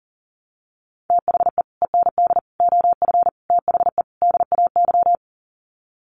Morse code, heard as a keyed tone, about 40 wpm, 700 Hz. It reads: THE END OF THE DAY
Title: Sets of 5 Words 40wpm